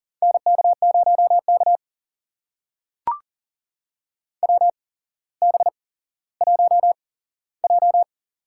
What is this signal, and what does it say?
Morse code, heard as a keyed tone, about 40 wpm, spaced as for 17 wpm, 700 Hz. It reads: NK0K WB1J
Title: USA Call Signs - 4 Characters - 8x Character Spacing - 40wpm